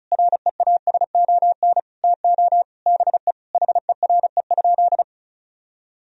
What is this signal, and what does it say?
Morse code, heard as a keyed tone, about 36 wpm, 700 Hz. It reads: REASON TO BE HERE?
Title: Sentences from Top 1000 Words - Review Part 4 - 35wpm